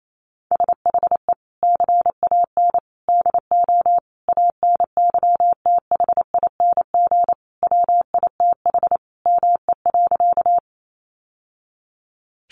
Morse code, heard as a keyed tone, about 28 wpm, 700 Hz. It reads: SHE CAN DO ANYTHING WITH ME.
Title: Sentences from Top 1000 Words 28wpm